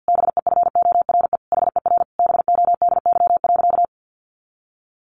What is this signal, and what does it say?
Morse code, heard as a keyed tone, about 50 wpm, 700 Hz. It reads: BEFORE HER BODY.
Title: Sentences from Top 1500 Words 50wpm